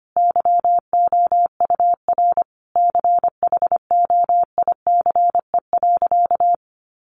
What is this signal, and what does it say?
Morse code, heard as a keyed tone, about 25 wpm, 700 Hz. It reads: YOUR CHOICE.
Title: Sentences from Top 900 Words 25wpm